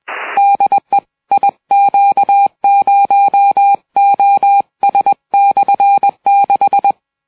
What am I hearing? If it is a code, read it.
Morse code, heard as a keyed tone, about 21 wpm, 800 Hz. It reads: DE IQ0OS/6
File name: iw0ffk / 70cm / 2008 10 05 dk1kc p jn58qh 610km cw 50w